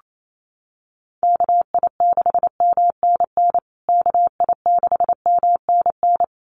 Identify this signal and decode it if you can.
Morse code, heard as a keyed tone, about 28 wpm, 700 Hz. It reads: KI6MNN KI6MNN
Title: USA Call Signs Repeated 2x 28wpm